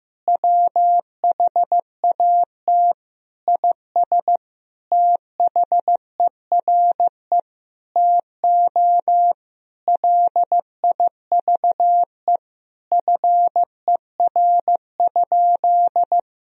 Morse code, heard as a keyed tone, about 15 wpm, 700 Hz. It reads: WHAT IS THERE TO LIVE FER?
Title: Sentences from Top 200 Words and Abbreviations 15wpm